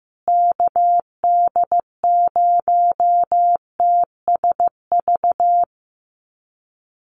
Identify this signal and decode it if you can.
Morse code, heard as a keyed tone, about 15 wpm, 700 Hz. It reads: KD0TSV
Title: USA Call Signs Repeated 2x 15wpm